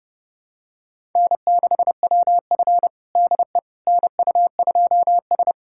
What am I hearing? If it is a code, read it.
Morse code, heard as a keyed tone, about 30 wpm, 700 Hz. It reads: N6WF DE NU2S
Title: USA Callsign DE Callsign - 4 Characters - 30wpm